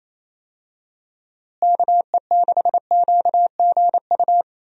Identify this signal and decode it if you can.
Morse code, heard as a keyed tone, about 28 wpm, 700 Hz. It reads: KE6QGU
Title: USA Call Signs - 6 Characters - Repeated 3x - 28wpm